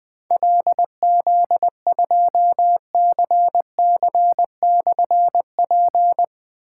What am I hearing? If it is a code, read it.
Morse code, heard as a keyed tone, about 21 wpm, 700 Hz. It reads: LZ2CC/P
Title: Difficult Call Signs 20wpm